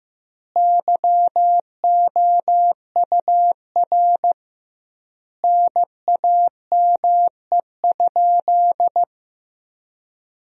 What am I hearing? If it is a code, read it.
Morse code, heard as a keyed tone, about 14 wpm, 700 Hz. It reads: YOUR NAME?
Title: 2x Word-Spacing - Sentences from Top 200 Words - 15wpm